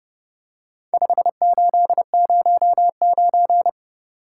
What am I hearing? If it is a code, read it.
Morse code, heard as a keyed tone, about 29 wpm, 700 Hz. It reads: 5809